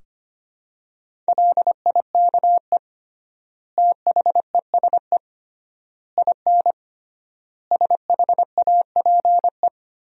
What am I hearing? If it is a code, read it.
Morse code, heard as a keyed tone, about 25 wpm, 700 Hz. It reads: LIKE THESE IN SHAPE
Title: 3x Word-Spacing - Sentences from Top 500 Words - 25wpm